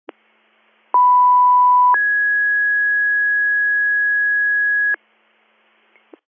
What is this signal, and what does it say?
Two-tone page, heard as a one-second tone, then a three-second tone, about 990 then 1650 Hz. It is heard from 0.9 to 4.9 s.